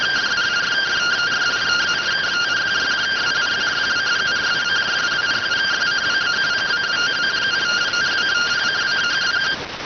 Two-tone teleprinter signal, about 170 Hz shift.